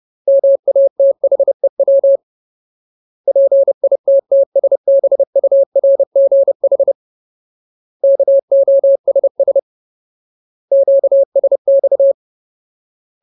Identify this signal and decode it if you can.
Morse code, heard as a keyed tone, about 30 wpm, 550 Hz. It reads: MATHEW PITTSBURGH KOSS QSX
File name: Ultimate Word List / 30WPM Ultimate